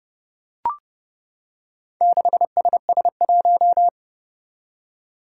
Morse code, heard as a keyed tone, about 30 wpm, 700 Hz. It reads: 6SS1
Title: Four Challenging Letter-Number Combinations 30wpm